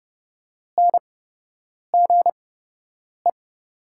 Morse code, heard as a keyed tone, about 30 wpm, spaced as for 8 wpm, 700 Hz. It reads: NGE
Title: 8x Character Spacing - Top 200 Words - 30wpm